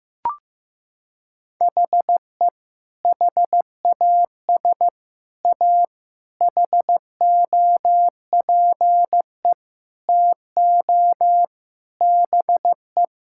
Morse code, heard as a keyed tone, about 15 wpm, 700 Hz. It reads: HE HAS A HOPE TO BE
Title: Sentences from Top 1000 Words - Review Part 6 - 15wpm